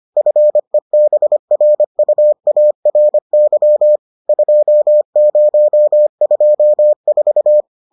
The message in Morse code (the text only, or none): FEBRUARY 2024